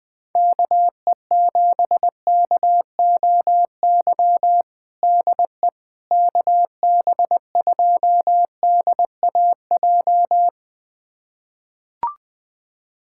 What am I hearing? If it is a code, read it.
Morse code, heard as a keyed tone, about 20 wpm, 700 Hz. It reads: KE7KOY DE KB2DAJ
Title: USA Callsign DE Callsign - 6 Characters - 20wpm